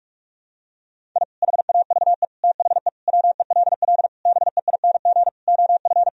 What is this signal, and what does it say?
Morse code, heard as a keyed tone, about 46 wpm, 700 Hz. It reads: I HAVE THE WELL BEING OF
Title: Sentences from Top 1000 Words - Review Part 3 - 45wpm